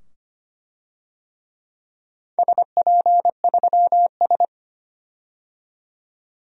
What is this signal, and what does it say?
Morse code, heard as a keyed tone, about 25 wpm, 700 Hz. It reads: SP3S
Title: DXCC-100 Call Signs - 4 Characters - 25wpm